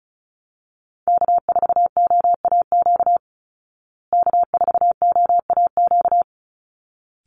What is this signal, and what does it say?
Morse code, heard as a keyed tone, about 34 wpm, 700 Hz. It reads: K4OAQ K4OAQ
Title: Call Signs Repeated 3x - 35wpm